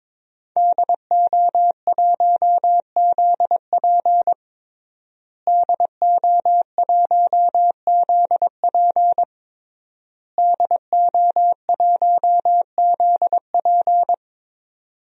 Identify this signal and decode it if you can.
Morse code, heard as a keyed tone, about 22 wpm, 700 Hz. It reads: DO1ZP DO1ZP DO1ZP
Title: DXCC-100 Call Signs - 5 Characters - Repeated 3x - 22wpm